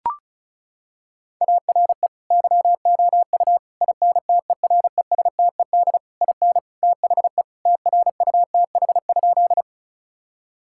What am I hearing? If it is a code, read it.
Morse code, heard as a keyed tone, about 34 wpm, 700 Hz. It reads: ARE YOU INTERESTED IN THE TRUTH?
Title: Sentences from Top 1000 Words - Review Part 6 - 35wpm